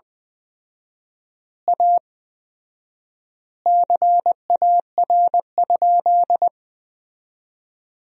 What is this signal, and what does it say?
Morse code, heard as a keyed tone, about 20 wpm, 700 Hz. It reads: A CAR?
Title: 4x Word-Spacing - Sentences from Top 300 Words - 20wpm